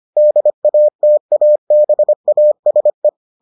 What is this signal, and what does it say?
Morse code, heard as a keyed tone, about 25 wpm, 600 Hz. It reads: DATABASE